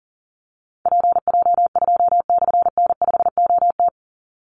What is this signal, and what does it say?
Morse code, heard as a keyed tone, about 40 wpm, 700 Hz. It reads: PJ2/N5OT
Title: Call Signs - 40wpm